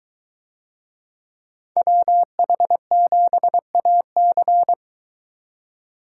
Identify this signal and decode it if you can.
Morse code, heard as a keyed tone, about 22 wpm, 700 Hz. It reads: WH7AC